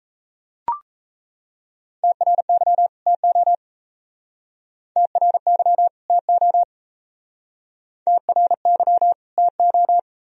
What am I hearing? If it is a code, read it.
Morse code, heard as a keyed tone, about 37 wpm, 700 Hz. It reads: TRY TO TRY TO TRY TO